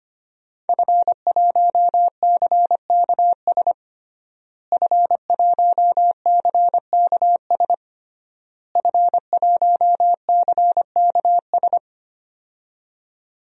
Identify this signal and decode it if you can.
Morse code, heard as a keyed tone, about 25 wpm, 700 Hz. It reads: F1CKS F1CKS F1CKS
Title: DXCC-100 Call Signs - 5 Characters - Repeated 3x - 25wpm